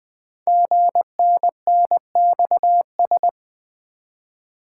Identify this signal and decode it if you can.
Morse code, heard as a keyed tone, about 20 wpm, 700 Hz. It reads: GNNXS
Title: Code Groups - 5 Letters - 20wpm